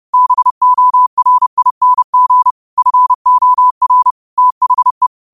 Morse code, heard as a keyed tone, about 30 wpm, 1000 Hz. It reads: BORING FOR THE
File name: 30wpm